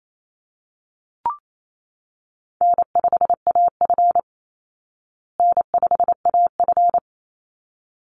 Morse code, heard as a keyed tone, about 28 wpm, 700 Hz. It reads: N5AF N5AF
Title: Easy Call Signs Repeated 2x - 28wpm